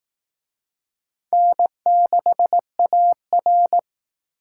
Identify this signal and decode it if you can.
Morse code, heard as a keyed tone, about 19 wpm, 700 Hz. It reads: N6AR
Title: Speed Racing - USA 1x2 Call Signs 15wpm